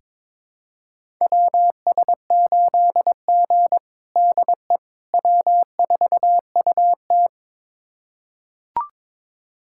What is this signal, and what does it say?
Morse code, heard as a keyed tone, about 22 wpm, 700 Hz. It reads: WS8G DE W4UT
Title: USA Callsign DE Callsign - 4 Characters - 22wpm